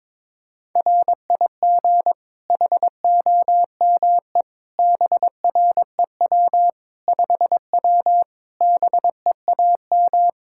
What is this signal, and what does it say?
Morse code, heard as a keyed tone, about 22 wpm, 700 Hz. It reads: RIG HOME BREW 5W BEAM
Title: Practice QSO - Exchange 2 - Info - 22wpm